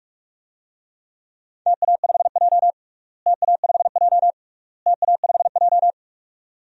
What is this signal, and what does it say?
Morse code, heard as a keyed tone, about 43 wpm, 700 Hz. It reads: TA5J TA5J TA5J
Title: DXCC-100 Call Signs - 4 Characters - Repeated 3x - 45wpm